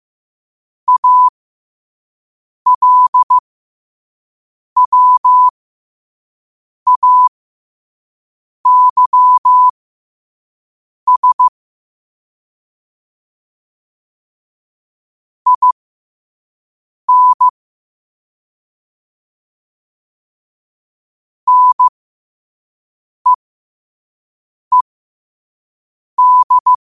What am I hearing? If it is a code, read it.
Morse code, heard as a keyed tone, about 15 wpm, spaced as for 5 wpm, 1000 Hz. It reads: ALWAYS IN NEED
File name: qotdfiles / 5wpm